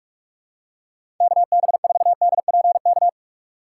Morse code, heard as a keyed tone, about 42 wpm, 700 Hz. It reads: KB4DPK
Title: USA Call Signs - 2x Character Spacing - 45wpm